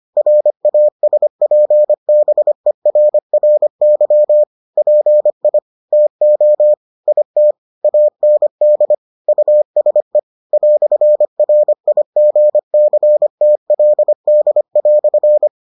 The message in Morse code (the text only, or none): RASPBERRY PI TO IT AND USE "RIGCTLD"